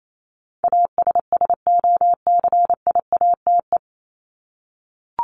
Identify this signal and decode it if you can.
Morse code, heard as a keyed tone, about 28 wpm, 700 Hz. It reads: ASSOCIATE
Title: Top 1751-2000 Words 28wpm